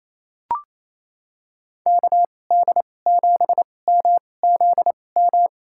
Morse code, heard as a keyed tone, about 28 wpm, spaced as for 20 wpm, 700 Hz. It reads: KD7MZM
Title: USA Call Signs - 6 Characters - 2x Character Spacing - 28wpm